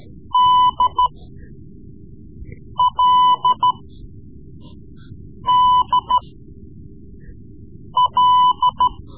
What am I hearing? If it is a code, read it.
Morse code, heard as a keyed tone, about 13 wpm, spaced as for 4 wpm, 1000 Hz. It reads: DLDL
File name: Morse Lessons / Morse Lessons / Tape-3B